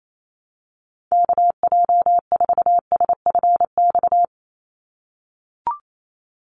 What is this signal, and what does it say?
Morse code, heard as a keyed tone, about 28 wpm, 700 Hz. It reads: KJ4SFX
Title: USA Call Signs 28wpm